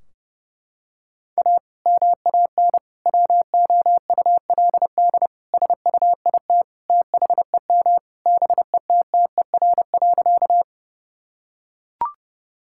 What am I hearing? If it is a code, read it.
Morse code, heard as a keyed tone, about 31 wpm, 700 Hz. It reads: A MAN WOULD SUIT THEM BETTER.
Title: Sentences from Top 1600 Words 30wpm